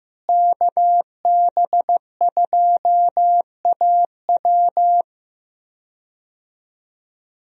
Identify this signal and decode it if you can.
Morse code, heard as a keyed tone, about 15 wpm, 700 Hz. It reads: KB2AW
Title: USA Call Signs - 5 Characters - Repeated 3x - 15wpm